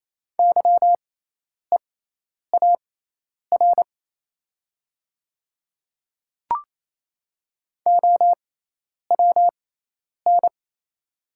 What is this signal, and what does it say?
Morse code, heard as a keyed tone, about 28 wpm, spaced as for 10 wpm, 700 Hz. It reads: YEAR OWN
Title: 6x Character Spacing - Top 300 Words - 28wpm